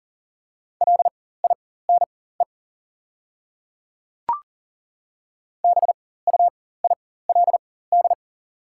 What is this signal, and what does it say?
Morse code, heard as a keyed tone, about 39 wpm, spaced as for 18 wpm, 700 Hz. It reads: LINE BUILD
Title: 4x Character Spacing - Top 200 Words - 40wpm